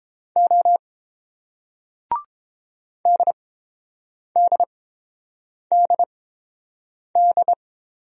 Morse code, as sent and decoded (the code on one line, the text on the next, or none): --- / -.. -.. -.. -..
O DDDD